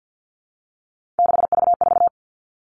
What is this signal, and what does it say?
Morse code, heard as a keyed tone, about 50 wpm, 700 Hz. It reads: 6V4
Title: Three Challenging Letter-Number Combinations 50wpm